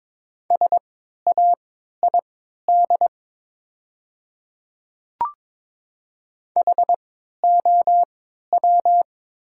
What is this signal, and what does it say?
Morse code, heard as a keyed tone, about 22 wpm, spaced as for 12 wpm, 700 Hz. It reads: SAID HOW
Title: 3x Character Spacing - Top 100 Words - 22wpm